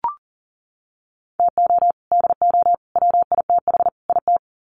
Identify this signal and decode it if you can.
Morse code, heard as a keyed tone, about 40 wpm, 700 Hz. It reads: TO DO WITH IT
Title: Sets of 4 Words 40wpm